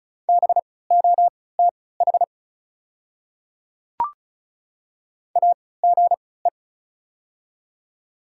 Morse code, heard as a keyed tone, about 36 wpm, spaced as for 20 wpm, 690 Hz. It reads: BOTH AGE
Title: 3x Character Spacing - Top 500 Words - 35wpm